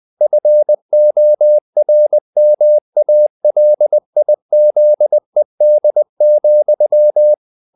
Morse code, heard as a keyed tone, about 20 wpm, 600 Hz. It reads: FORMALIZED,
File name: the old man and the sea / the old man and the sea part 1 20wpm